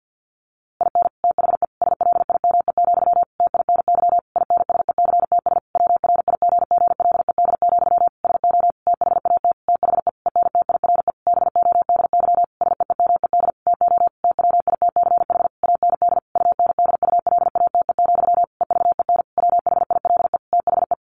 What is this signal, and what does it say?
Morse code, heard as a keyed tone, about 50 wpm, 700 Hz. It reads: IN THE SLIME, TINY INSECTS WRIGGLED, SO THAT THE ENTIRE BODY SEEMED TO TWITCH AND UNDULATE, EVEN WHILE THE
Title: A Sound Of Thunder 50wpm